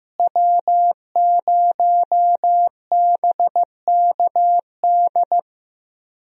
Morse code, heard as a keyed tone, about 15 wpm, 700 Hz. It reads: W0BKD
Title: USA Call Signs - 5 Characters - Repeated 2x - 15wpm